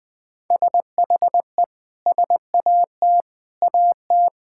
Morse code, heard as a keyed tone, about 20 wpm, 700 Hz. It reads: SHE SAT AT
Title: Sentences with 4-Word Phrases - 20wpm